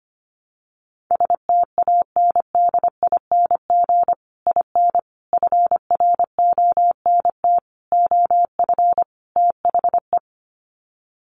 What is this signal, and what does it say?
Morse code, heard as a keyed tone, about 25 wpm, 700 Hz. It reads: STANDING IN FRONT OF THE
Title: Sets of 5 Words Encore2 25wpm